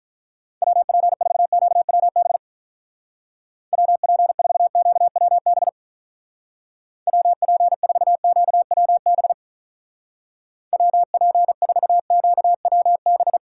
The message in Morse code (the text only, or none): WP4QWB WP4QWB WP4QWB WP4QWB